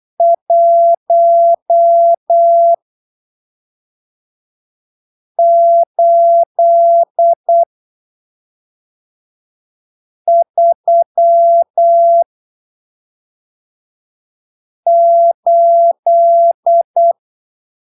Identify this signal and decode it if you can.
Morse code, heard as a keyed tone, about 8 wpm, spaced as for 3 wpm, 680 Hz. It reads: 1838